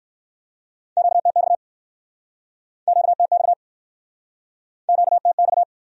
Morse code, heard as a keyed tone, about 62 wpm, 700 Hz. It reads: QTX QTX QTX